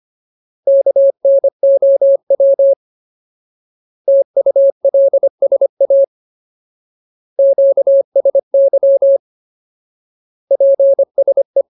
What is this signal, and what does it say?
Morse code, heard as a keyed tone, about 25 wpm, 550 Hz. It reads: KNOW TULSA QSY PSE
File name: Ultimate Word List / 25WPM Ultimate